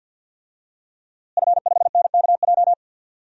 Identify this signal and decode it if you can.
Morse code, heard as a keyed tone, about 50 wpm, 700 Hz. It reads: F5NKJ
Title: DXCC-100 Call Signs - 5 Characters - Repeated 3x - 50wpm